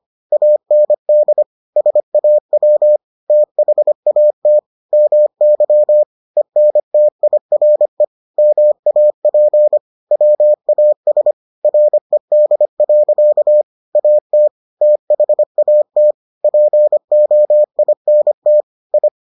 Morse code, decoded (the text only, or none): AND SAW THAT MY ENTIRE MAP WAS RED. AT THAT POINT I